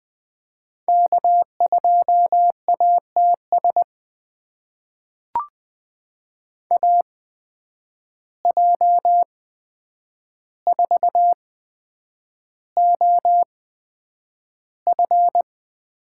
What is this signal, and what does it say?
Morse code, heard as a keyed tone, about 20 wpm, spaced as for 9 wpm, 700 Hz. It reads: K2ATS AJ4OF
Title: USA Call Signs - 5 Characters - 8x Character Spacing - 20wpm